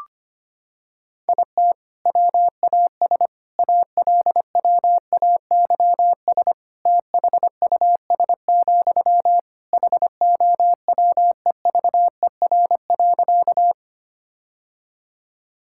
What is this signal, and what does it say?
Morse code, heard as a keyed tone, about 25 wpm, 700 Hz. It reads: IT WAS ALWAYS THUS, HOWEVER.
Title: Sentences from Top 1000 Words - Review Part 5 - 25wpm